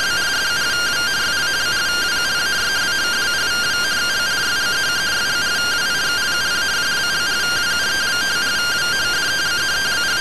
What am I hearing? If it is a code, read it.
Two-tone teleprinter signal, about 170 Hz shift.
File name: findings / 81 axial iodine 262.gen-19 rtty rx